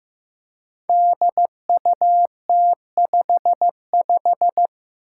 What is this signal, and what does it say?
Morse code, heard as a keyed tone, about 15 wpm, 700 Hz. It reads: DUT55
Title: Five Challenging Letter-Number Combinations 15wpm